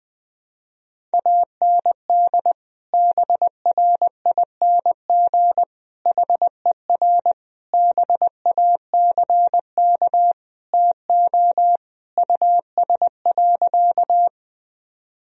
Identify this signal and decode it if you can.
Morse code, heard as a keyed tone, about 20 wpm, 700 Hz. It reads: AND BRING HER BACK TO US.
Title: Sentences from Top 1000 Words - Review Part 3 - 20wpm